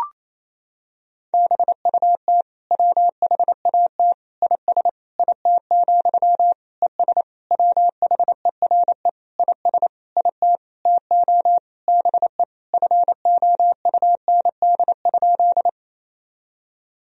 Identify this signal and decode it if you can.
Morse code, heard as a keyed tone, about 28 wpm, 700 Hz. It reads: BUT WHAT IS IT, ES WHERE IS IT TO BE FOUND?
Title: Sentences from Top 300 Words and Abbreviations 28wpm